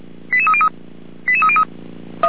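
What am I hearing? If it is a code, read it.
Five-tone selective call (ZVEI), heard as quick runs of five tones, tones of about 70 ms. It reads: 88303 88303
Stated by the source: ZVEI five-tone selective call received by SDR, recorded on the audio server of Feuerwehr Orlamünde.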